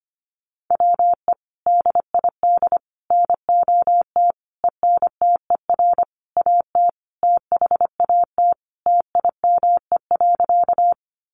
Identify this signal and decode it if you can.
Morse code, heard as a keyed tone, about 25 wpm, 700 Hz. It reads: WE DID NOT ENTER AT THAT TIME.
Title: Sentences from Top 1300 Words 25wpm